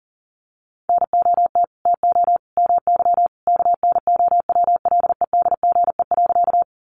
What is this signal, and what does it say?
Morse code, heard as a keyed tone, about 40 wpm, 700 Hz. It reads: NOT TO MY KNOWLEDGE.
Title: Sentences from Top 600 Words 40wpm